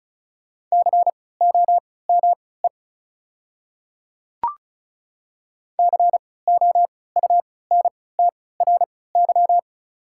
Morse code, heard as a keyed tone, about 34 wpm, spaced as for 20 wpm, 690 Hz. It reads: COME COUNTRY
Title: 3x Character Spacing - Top 300 Words - 35wpm